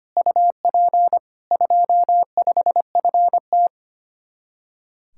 Morse code, heard as a keyed tone, about 32 wpm, 700 Hz. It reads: UP 25FT